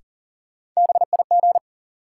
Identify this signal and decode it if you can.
Morse code, heard as a keyed tone, about 40 wpm, 700 Hz. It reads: BIG
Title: Sentences from Top 200 Words - New Word Drill - 40wpm